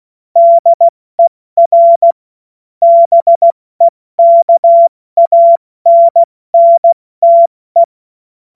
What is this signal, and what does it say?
Morse code, heard as a keyed tone, about 16 wpm, 680 Hz. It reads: DER BEKANNTE
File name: Morsekurs / Witze / Witz 29 April 2025 70 F80